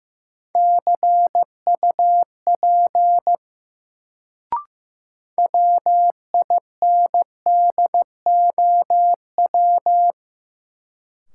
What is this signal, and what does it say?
Morse code, heard as a keyed tone, about 15 wpm, 700 Hz. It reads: CUP WINDOW
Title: Sentences from Top 1600 Words - New Word Drill - 15wpm